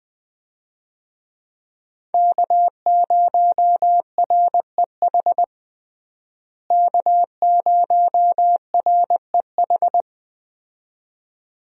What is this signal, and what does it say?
Morse code, heard as a keyed tone, about 20 wpm, 700 Hz. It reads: K0REH K0REH
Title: USA Call Signs - 5 Characters - Repeated 2x - 20wpm